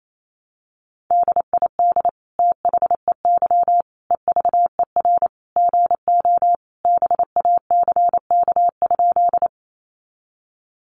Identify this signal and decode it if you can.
Morse code, heard as a keyed tone, about 28 wpm, 700 Hz. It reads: DID THEY EVER GO BACK?
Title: Sentences from Top 400 Words 28wpm